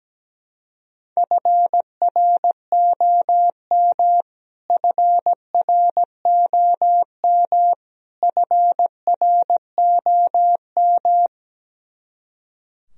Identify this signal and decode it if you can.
Morse code, heard as a keyed tone, about 17 wpm, 700 Hz. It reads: FROM FROM FROM
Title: Top 100 Words Repeated 17wpm